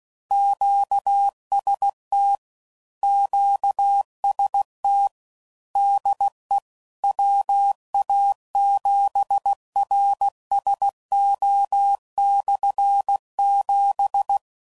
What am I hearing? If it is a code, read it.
Morse code, heard as a keyed tone, about 16 wpm, 800 Hz. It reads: QST QST DE WA7RSO/7